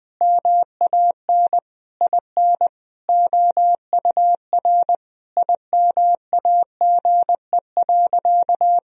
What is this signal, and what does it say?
Morse code, heard as a keyed tone, about 21 wpm, 700 Hz. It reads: MAN IN OUR IMAGE.
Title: Sentences from Top 1000 Words - Review Part 4 - 20wpm